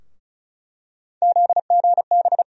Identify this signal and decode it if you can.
Morse code, heard as a keyed tone, about 34 wpm, 700 Hz. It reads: ZGB
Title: USA Call Sign Postfixes - 3 Letters 35wpm